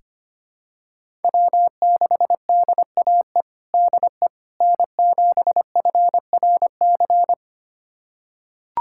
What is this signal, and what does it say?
Morse code, heard as a keyed tone, about 25 wpm, 700 Hz. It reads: W6DAE DE N7FRC
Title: USA Callsign DE Callsign - 5 Characters - 25wpm